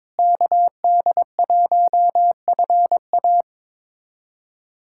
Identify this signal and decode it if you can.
Morse code, heard as a keyed tone, about 22 wpm, 700 Hz. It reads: KD1FA
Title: USA Callsign DE Callsign - 5 Characters - 22wpm